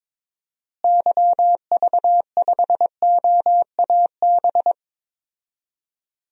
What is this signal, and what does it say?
Morse code, heard as a keyed tone, about 22 wpm, 700 Hz. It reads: YV5OAB
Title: DXCC-100 Call Signs - 6 Characters - Repeated 3x - 22wpm